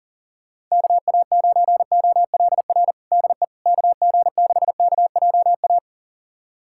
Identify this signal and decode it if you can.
Morse code, heard as a keyed tone, about 40 wpm, 700 Hz. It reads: KA9OLR DE KG6KJA
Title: USA Callsign DE Callsign - 6 Characters - 40wpm